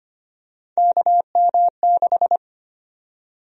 Morse code, heard as a keyed tone, about 25 wpm, 700 Hz. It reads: KM6
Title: USA Call Sign Prefixes - 25wpm